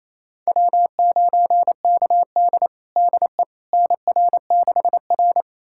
Morse code, heard as a keyed tone, about 28 wpm, 700 Hz. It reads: W9KD DE NR6R
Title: USA Callsign DE Callsign - 4 Characters - 28wpm